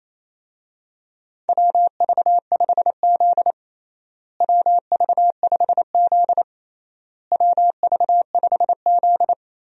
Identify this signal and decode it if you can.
Morse code, heard as a keyed tone, about 28 wpm, 700 Hz. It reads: WV5Z WV5Z WV5Z
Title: USA Call Signs - 4 Characters - Repeated 3x - 28wpm